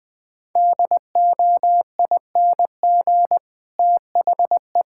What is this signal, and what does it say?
Morse code, heard as a keyed tone, about 21 wpm, 700 Hz. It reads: DOING THE
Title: Sets of 9 Words - 20wpm